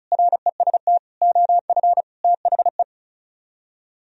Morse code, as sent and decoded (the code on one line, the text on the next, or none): .-. ... - / --- ..-. / - .... .
REST OF THE